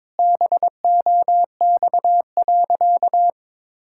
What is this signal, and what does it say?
Morse code, heard as a keyed tone, about 22 wpm, 700 Hz. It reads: BOX.
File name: practice / 22wpm / Sentences with 3-Word Phrases - 22wpm